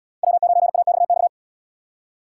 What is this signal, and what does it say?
Morse code, heard as a keyed tone, about 77 wpm, 700 Hz. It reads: W9AXX